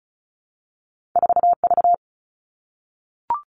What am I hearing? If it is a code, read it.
Morse code, heard as a keyed tone, about 36 wpm, 700 Hz. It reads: 4V
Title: All Cut Numbers - 2-Digit 35wpm